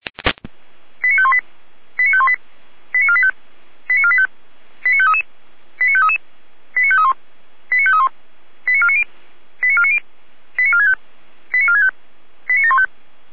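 Five-tone selective call (ZVEI), heard as quick runs of five tones, tones of about 70 ms. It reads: 89518 89518 89485 89485 89533 89533 89531 89531 89490 89490 89465 89465 89715
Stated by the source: ZVEI five-tone selective call received on SDR.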